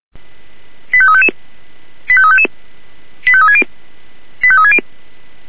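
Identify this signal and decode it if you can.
Five-tone selective call (ZVEI), heard as quick runs of five tones, tones of about 70 ms. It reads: 85355 85355 85360 85360
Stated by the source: ZVEI five-tone selective call received by SDR.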